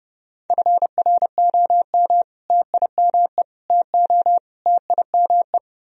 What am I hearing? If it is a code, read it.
Morse code, heard as a keyed tone, about 34 wpm, 700 Hz. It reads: FROM TIME TO TIME